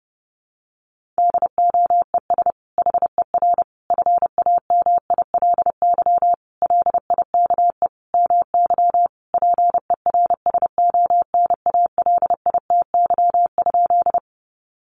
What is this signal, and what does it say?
Morse code, heard as a keyed tone, about 30 wpm, 700 Hz. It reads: DOES HER FAMILY LIKE MY PERSONALITY?
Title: Sentences from Top 1400 Words 30wpm